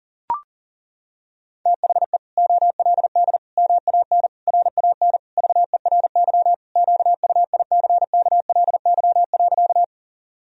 Morse code, heard as a keyed tone, about 40 wpm, 700 Hz. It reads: THE OLD MAN RAN VERY QUICKLY.
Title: Sentences from Top 1000 Words - Review Part 3 - 40wpm